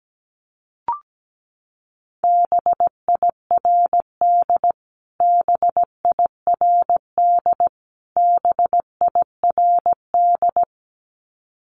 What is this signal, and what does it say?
Morse code, heard as a keyed tone, about 17 wpm, 700 Hz. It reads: BIRD BIRD BIRD
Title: Top 500 Words Repeated 17wpm